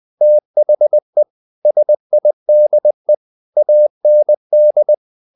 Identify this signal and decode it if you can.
Morse code, heard as a keyed tone, about 20 wpm, 600 Hz. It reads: THE SIDE AND
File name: the old man and the sea part 2 20wpm